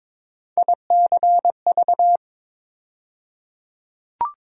Morse code, heard as a keyed tone, about 22 wpm, 700 Hz. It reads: ICV
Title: USA Call Sign Postfixes - 3 Letters 22wpm